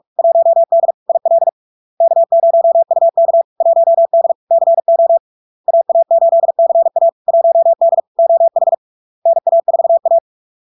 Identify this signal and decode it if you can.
Morse code, heard as a keyed tone, about 46 wpm, 670 Hz. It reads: 1D IL K0UK 1D CO AA8CA 1D OH NA4A
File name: LARK-FD-2022-Log-in-CW-45wpm-670-Hz